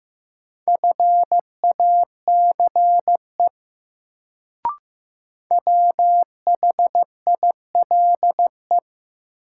Morse code, heard as a keyed tone, about 15 wpm, 700 Hz. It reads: FACE WHILE
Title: Sets of 3 Words - New Word Drill - 15wpm